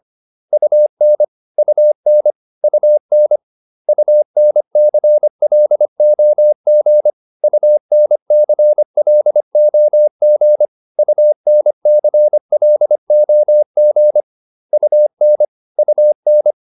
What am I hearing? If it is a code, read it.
Morse code, heard as a keyed tone, about 25 wpm, 600 Hz. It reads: UN UN UN UNCLOG UNCLOG UNCLOG UN UN